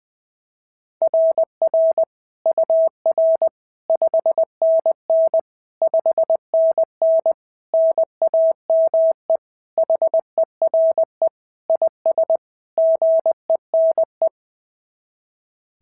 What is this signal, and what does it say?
Morse code, heard as a keyed tone, about 20 wpm, 690 Hz. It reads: RR UR 5NN 5NN NAME HERE IS GENE
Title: SOTA QSO Practice 20wpm